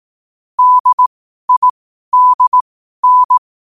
Morse code, heard as a keyed tone, about 18 wpm, spaced as for 12 wpm, 1000 Hz. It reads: DIDN
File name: qotdfiles / 12wpm